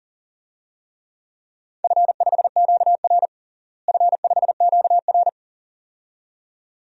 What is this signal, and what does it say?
Morse code, heard as a keyed tone, about 37 wpm, 700 Hz. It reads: F5QR F5QR
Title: DXCC-100 Call Signs - 4 Characters - Repeated 2x - 40wpm